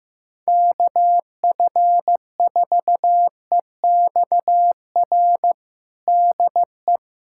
Morse code, heard as a keyed tone, about 15 wpm, 700 Hz. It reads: KF4EXR DE
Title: USA Callsign DE Callsign 15wpm